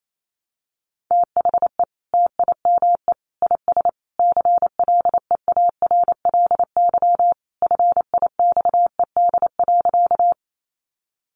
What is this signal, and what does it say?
Morse code, heard as a keyed tone, about 28 wpm, 700 Hz. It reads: THE TIME IS CLEARLY FIXED.
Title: Sentences from Top 1300 Words 28wpm